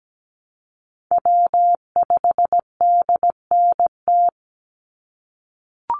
Morse code, heard as a keyed tone, about 17 wpm, 700 Hz. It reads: W5DNT
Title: Call Signs - 17wpm